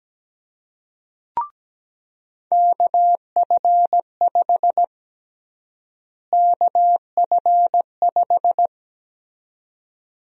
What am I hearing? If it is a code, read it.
Morse code, heard as a keyed tone, about 17 wpm, 700 Hz. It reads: KF5 KF5
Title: USA Call Sign Prefixes - Repeated 2x - 17wpm